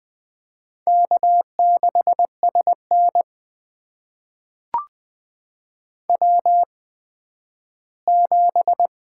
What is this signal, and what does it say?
Morse code, heard as a keyed tone, about 20 wpm, spaced as for 12 wpm, 700 Hz. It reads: K6SN W7
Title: USA Call Signs - 4 Characters - 8x Character Spacing - 20wpm